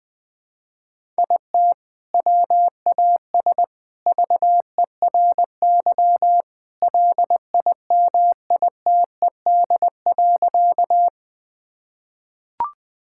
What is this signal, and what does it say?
Morse code, heard as a keyed tone, about 20 wpm, 700 Hz. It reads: IT WAS VERY LIMITED.
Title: Sentences from Top 1000 Words - Review Part 6 - 20wpm